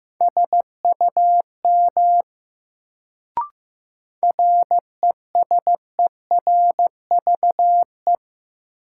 Morse code, heard as a keyed tone, about 15 wpm, 700 Hz. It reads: SUM RESERVE
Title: Sentences from Top 1000 Words - Review Part 6 - New Word Drill - 15wpm